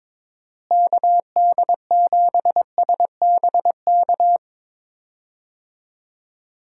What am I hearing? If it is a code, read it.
Morse code, heard as a keyed tone, about 22 wpm, 700 Hz. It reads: KD7SBK
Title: USA Call Signs - 6 Characters - Repeated 3x - 22wpm